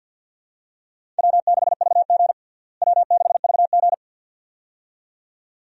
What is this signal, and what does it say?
Morse code, heard as a keyed tone, about 51 wpm, 700 Hz. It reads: W6VG W6VG
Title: USA Call Signs - 4 Characters - Repeated 2x - 50wpm